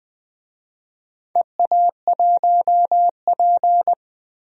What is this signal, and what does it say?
Morse code, heard as a keyed tone, about 21 wpm, 700 Hz. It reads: EA1P
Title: DXCC-100 Call Signs - 4 Characters - 20wpm